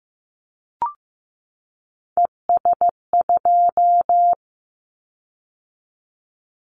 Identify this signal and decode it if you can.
Morse code, heard as a keyed tone, about 15 wpm, 700 Hz. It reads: ES2
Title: Callsign Prefixes for Top 100 DXCC - 15wpm